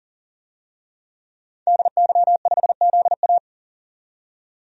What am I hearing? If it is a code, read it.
Morse code, heard as a keyed tone, about 40 wpm, 700 Hz. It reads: DY5ZA
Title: DXCC-100 Call Signs - 5 Characters - 40wpm